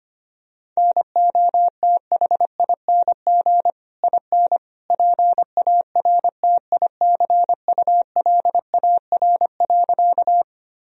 Morse code, heard as a keyed tone, about 25 wpm, 700 Hz. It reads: NOTHING IN PARTICULAR.
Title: Sentences from Top 1000 Words - Review Part 4 - 25wpm